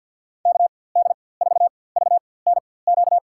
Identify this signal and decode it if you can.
Morse code, heard as a keyed tone, about 53 wpm, spaced as for 24 wpm, 700 Hz. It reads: KD4VNQ
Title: USA Call Signs - 6 Characters - 4x Character Spacing - 50wpm